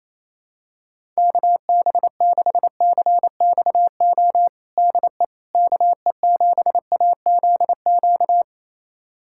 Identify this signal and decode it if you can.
Morse code, heard as a keyed tone, about 28 wpm, 700 Hz. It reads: KB6CXO DE KE7AZQ
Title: USA Callsign DE Callsign - 6 Characters - 28wpm